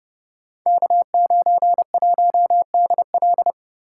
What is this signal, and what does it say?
Morse code, heard as a keyed tone, about 30 wpm, 700 Hz. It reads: K91DL
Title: Code Groups - 5 Letter:Numbers - 30wpm